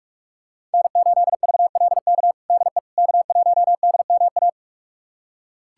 Morse code, heard as a keyed tone, about 42 wpm, 700 Hz. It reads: N8VLK DE K1DMA
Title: USA Callsign DE Callsign - 5 Characters - 45wpm